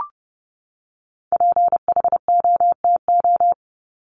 Morse code, heard as a keyed tone, about 29 wpm, 700 Hz. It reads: PHOTO
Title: Top 1001-1250 Words 30wpm